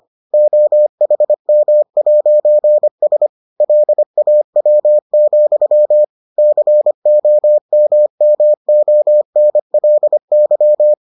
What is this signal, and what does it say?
Morse code, heard as a keyed tone, about 25 wpm, 600 Hz. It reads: OHM'S LAW, COMMONLY